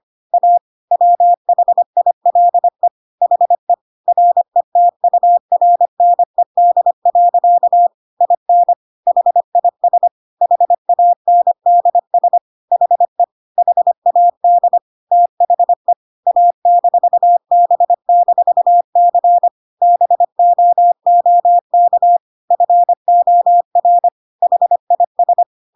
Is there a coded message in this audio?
Yes: Morse code from 0.3 to 25.4 s, heard as a keyed tone, about 25 wpm, 700 Hz. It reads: A WHILE HE RETURNED. IN HIS HANDS HE HAD THE A-B-C BOOK FOR HIS